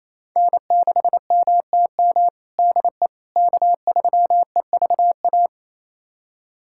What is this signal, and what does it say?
Morse code, heard as a keyed tone, about 28 wpm, 700 Hz. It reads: N6MTM DE K3EVA
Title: USA Callsign DE Callsign - 5 Characters - 28wpm